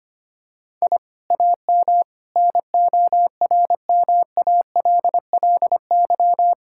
Morse code, heard as a keyed tone, about 25 wpm, 700 Hz. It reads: I AM NORMALLY
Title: Sentences from Top 1000 Words - Review Part 6 - 25wpm